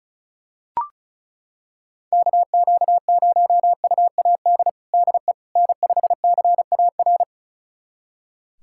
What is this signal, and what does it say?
Morse code, heard as a keyed tone, about 36 wpm, 700 Hz. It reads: KQ0UAD DE N5CAR
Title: USA Callsign DE Callsign 35wpm